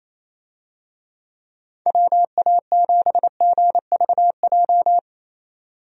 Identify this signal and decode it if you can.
Morse code, heard as a keyed tone, about 27 wpm, 700 Hz. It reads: WA7GVJ
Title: USA Call Signs - 6 Characters - 28wpm